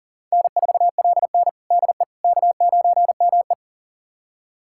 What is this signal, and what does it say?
Morse code, heard as a keyed tone, about 40 wpm, 700 Hz. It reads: N4LN DE K9ME
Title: USA Callsign DE Callsign - 4 Characters - 40wpm